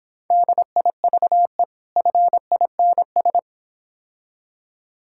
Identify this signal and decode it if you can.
Morse code, heard as a keyed tone, about 26 wpm, 700 Hz. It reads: DIVE FINS